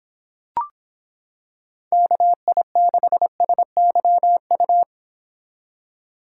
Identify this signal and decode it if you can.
Morse code, heard as a keyed tone, about 26 wpm, 700 Hz. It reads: KI6SYU